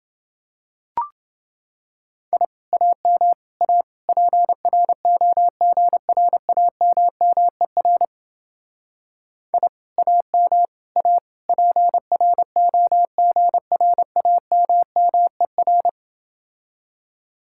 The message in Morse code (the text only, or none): I AM A PROGRAMMER I AM A PROGRAMMER